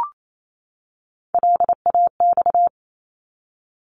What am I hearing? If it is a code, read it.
Morse code, heard as a keyed tone, about 28 wpm, 700 Hz. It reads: LAX
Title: ARRL Sections 28wpm